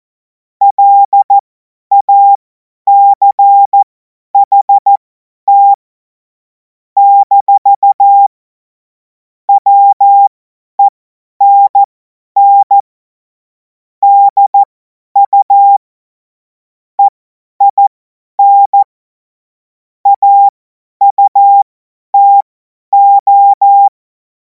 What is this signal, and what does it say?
Morse code, heard as a keyed tone, about 14 wpm, spaced as for 10 wpm, 800 Hz. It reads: LACHT - WENN DU EIN AUTO
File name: Morsekurs / Witze / Witz 21. April 2026 50 F70